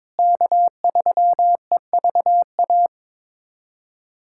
Morse code, heard as a keyed tone, about 22 wpm, 700 Hz. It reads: K3EVA
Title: USA Callsign DE Callsign - 5 Characters - 22wpm